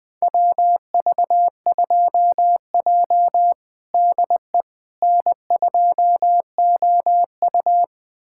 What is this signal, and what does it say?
Morse code, heard as a keyed tone, about 20 wpm, 700 Hz. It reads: WV2J DE N2OU
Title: USA Callsign DE Callsign - 4 Characters - 20wpm